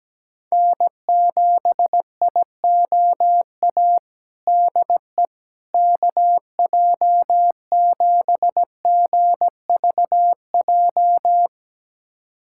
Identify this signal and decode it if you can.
Morse code, heard as a keyed tone, about 17 wpm, 700 Hz. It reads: N7IOA DE KJ7GVJ
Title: USA Callsign DE Callsign 17wpm